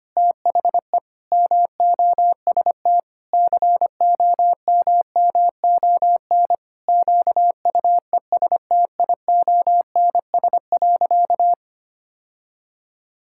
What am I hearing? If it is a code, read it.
Morse code, heard as a keyed tone, about 25 wpm, 700 Hz. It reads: THE MOST COMMON QUESTIONS.
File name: practice / 25wpm / Sentences with 4-Word Phrases - 25wpm